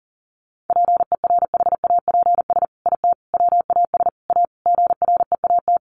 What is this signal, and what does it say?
Morse code, heard as a keyed tone, about 40 wpm, 700 Hz. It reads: PERHAPS IT WAS A GREAT
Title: Sentences from Top 1400 Words 40wpm